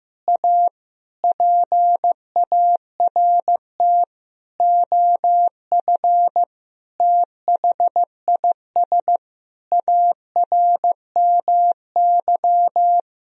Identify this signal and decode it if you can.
Morse code, heard as a keyed tone, about 15 wpm, 700 Hz. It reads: A PART OF THIS ARMY
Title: Sentences from Top 1000 Words - Review Part 2 - 15wpm